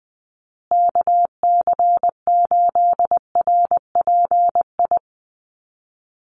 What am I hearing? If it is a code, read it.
Morse code, heard as a keyed tone, about 20 wpm, 700 Hz. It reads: KC8RPI
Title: USA Call Signs Repeated 2x 20wpm